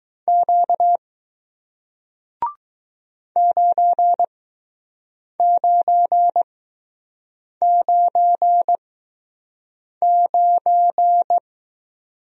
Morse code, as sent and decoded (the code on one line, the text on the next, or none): --.- / ----. ----. ----. ----.
Q 9999